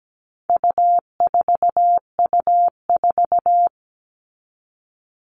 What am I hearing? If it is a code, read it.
Morse code, heard as a keyed tone, about 17 wpm, 700 Hz. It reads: U4U4
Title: All Cut Numbers - 4-Digit 17wpm